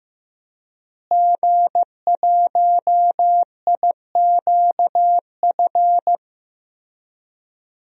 Morse code, heard as a keyed tone, about 16 wpm, 700 Hz. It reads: G1IQF
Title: DXCC-100 Call Signs - 5 Characters - Repeated 2x - 15wpm